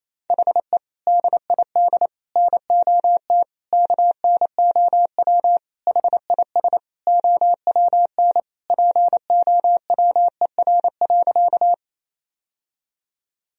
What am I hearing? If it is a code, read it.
Morse code, heard as a keyed tone, about 27 wpm, 700 Hz. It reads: HE DID NOT KNOW HIS OWN POWER.
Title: Sentences from Top 1000 Words - Review Part 2 - 28wpm